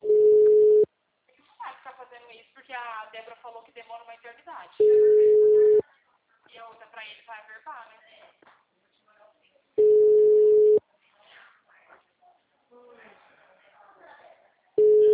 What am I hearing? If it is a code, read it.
Ringback tone (Europe, 425 Hz), with speech.
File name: out- -2010- - - .27